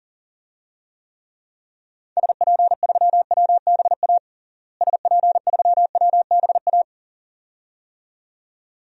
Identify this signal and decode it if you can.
Morse code, heard as a keyed tone, about 40 wpm, 700 Hz. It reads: SP3WBA SP3WBA
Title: DXCC-100 Call Signs - 6 Characters - Repeated 2x - 40wpm